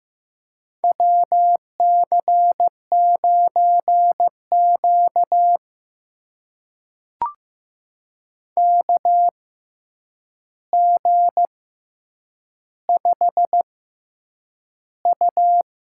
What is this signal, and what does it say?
Morse code, heard as a keyed tone, about 15 wpm, spaced as for 8 wpm, 700 Hz. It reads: WC9Q KG5U
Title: USA Call Signs - 6x Character Spacing - 15wpm